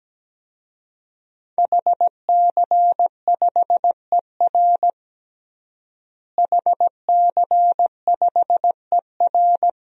Morse code, heard as a keyed tone, about 17 wpm, 700 Hz. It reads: HC5ER HC5ER
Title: DXCC-100 Call Signs - 5 Characters - Repeated 2x - 17wpm